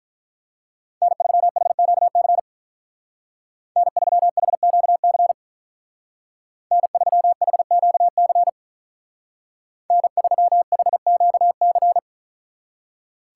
Morse code, heard as a keyed tone, about 44 wpm, 700 Hz. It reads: N3HQC N3HQC N3HQC N3HQC
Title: Speed Racing - USA 1x3 Call Signs 35wpm